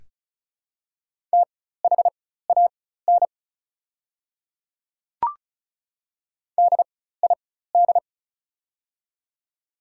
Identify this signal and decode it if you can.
Morse code, heard as a keyed tone, about 36 wpm, spaced as for 16 wpm, 690 Hz. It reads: THAN DID
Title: 4x Character Spacing - Top 300 Words - 35wpm